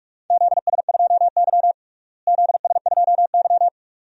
Morse code, heard as a keyed tone, about 41 wpm, 700 Hz. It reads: ZS2Y ZS2Y
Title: DXCC-100 Call Signs - 4 Characters - Repeated 2x - 45wpm